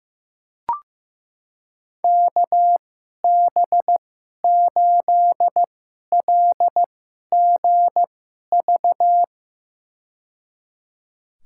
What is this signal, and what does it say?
Morse code, heard as a keyed tone, about 15 wpm, spaced as for 11 wpm, 700 Hz. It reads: KB8LGV
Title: USA Call Signs - 6 Characters - 2x Character Spacing - 15wpm